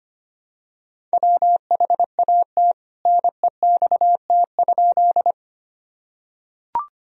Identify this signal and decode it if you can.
Morse code, heard as a keyed tone, about 25 wpm, 700 Hz. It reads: WHAT NEXT?
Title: Sentences from Top 1000 Words - Review Part 1 - 25wpm